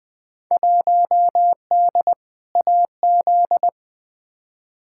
Morse code, heard as a keyed tone, about 20 wpm, 700 Hz. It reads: 1D AZ
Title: Field Day Exchanges - 20wpm